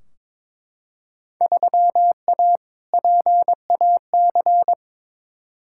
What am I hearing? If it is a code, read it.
Morse code, heard as a keyed tone, about 21 wpm, 700 Hz. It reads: 3A PAC
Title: Field Day Exchanges Encore - 22wpm